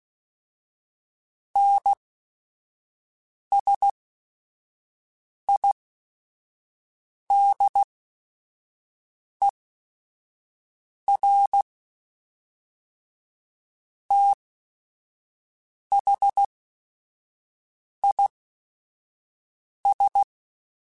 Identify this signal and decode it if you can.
Morse code, heard as a keyed tone, about 16 wpm, spaced as for 5 wpm, 800 Hz. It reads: NSIDER THIS
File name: Technical Subjects / Technical Subjects-mp3s / AND Gate @5